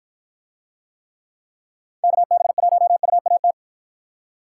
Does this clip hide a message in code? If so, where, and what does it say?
Yes: Morse code from 2.0 to 3.5 s, heard as a keyed tone, about 53 wpm, 700 Hz. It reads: KB1UAT